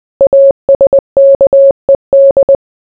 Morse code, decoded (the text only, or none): ASKED